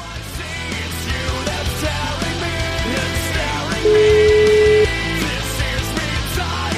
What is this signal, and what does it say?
Ringback tone (Europe, 425 Hz), with noise.